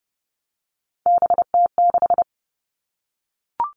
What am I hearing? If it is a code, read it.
Morse code, heard as a keyed tone, about 30 wpm, 700 Hz. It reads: BT6
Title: All Cut Numbers - 3-Digit 30wpm